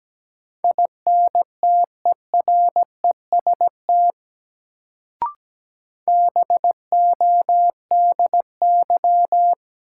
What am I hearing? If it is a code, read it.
Morse code, heard as a keyed tone, about 17 wpm, 700 Hz. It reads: INTEREST BODY
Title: Sentences from Top 400 Words - New Word Drill - 17wpm